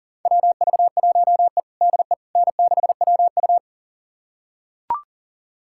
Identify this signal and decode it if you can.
Morse code, heard as a keyed tone, about 37 wpm, 700 Hz. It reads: WV1E DE N6WU
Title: USA Callsign DE Callsign - 4 Characters - 40wpm